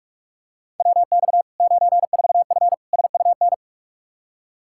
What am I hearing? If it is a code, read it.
Morse code, heard as a keyed tone, about 46 wpm, 700 Hz. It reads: WX 94F SUN
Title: Practice QSO - Exchange 2 - Info - 45wpm